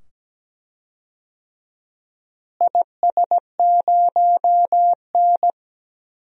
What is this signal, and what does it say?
Morse code, heard as a keyed tone, about 17 wpm, 700 Hz. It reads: IS0N
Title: DXCC-100 Call Signs - 4 Characters - Repeated 2x - 17wpm